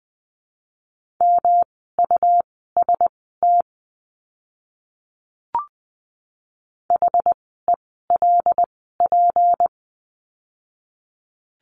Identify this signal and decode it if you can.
Morse code, heard as a keyed tone, about 20 wpm, spaced as for 15 wpm, 700 Hz. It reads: MUST HELP
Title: 237 Lesson 38f - Letter q - Words 20wpm